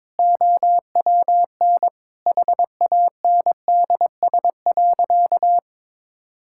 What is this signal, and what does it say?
Morse code, heard as a keyed tone, about 22 wpm, 700 Hz. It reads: OWN HANDS.
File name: practice / 22wpm / Sentences with 8-Word Phrases - 22wpm